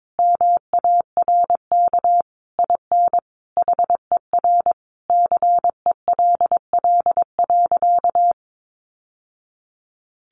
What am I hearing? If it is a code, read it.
Morse code, heard as a keyed tone, about 22 wpm, 700 Hz. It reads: MARK IN HER CELL.
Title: Sentences from Top 1100 Words 22wpm